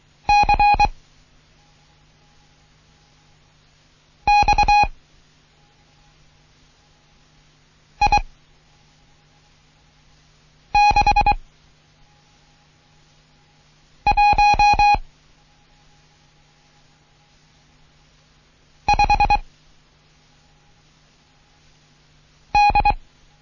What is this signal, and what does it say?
Morse code, heard as a keyed tone, about 23 wpm, 830 Hz. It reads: C X I 6 1 5 D
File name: Morsen / Kurs A 16